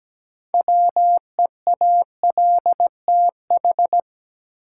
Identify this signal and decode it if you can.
Morse code, heard as a keyed tone, about 17 wpm, 700 Hz. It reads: WEALTH
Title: Sentences from Top 1000 Words - Review Part 4 - New Word Drill - 17wpm